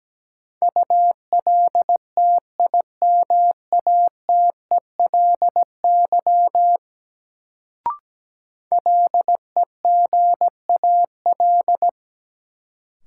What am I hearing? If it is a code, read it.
Morse code, heard as a keyed tone, about 17 wpm, 700 Hz. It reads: ULTIMATELY LEGAL
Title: Sentences from Top 1200 Words - New Word Drill Alt - 17wpm